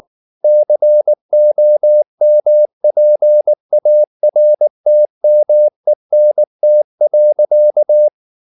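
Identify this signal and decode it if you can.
Morse code, heard as a keyed tone, about 19 wpm, 600 Hz. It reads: COMPARTMENT.
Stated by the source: Morse code at 19 wpm from CW med Gnister, reading Ken Follett's novel Never in English.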